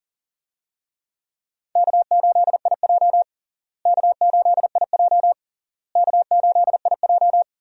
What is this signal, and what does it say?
Morse code, heard as a keyed tone, about 40 wpm, 700 Hz. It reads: K8IJ K8IJ K8IJ
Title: USA Call Signs - 4 Characters - Repeated 3x - 40wpm